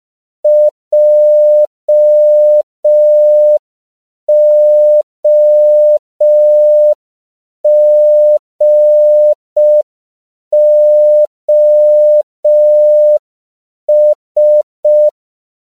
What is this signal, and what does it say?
Morse code, heard as a keyed tone, about 5 wpm, 600 Hz. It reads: JOGOS
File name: CW / 5PPM